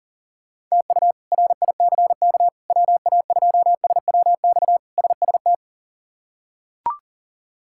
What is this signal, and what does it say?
Morse code, heard as a keyed tone, about 37 wpm, 700 Hz. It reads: TU RICK WA2SWX SST
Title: K1USN SST QSO Practice - 40wpm